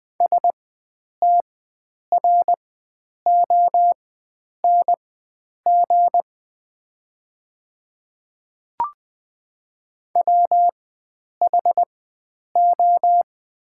Morse code, heard as a keyed tone, about 20 wpm, spaced as for 9 wpm, 700 Hz. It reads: STRONG WHO
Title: 4x Character Spacing - Top 500 Words - 20wpm